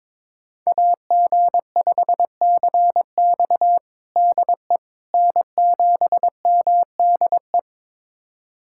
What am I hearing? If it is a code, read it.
Morse code, heard as a keyed tone, about 21 wpm, 700 Hz. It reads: AG5CX DE N7MDE
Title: USA Callsign DE Callsign - 5 Characters - 22wpm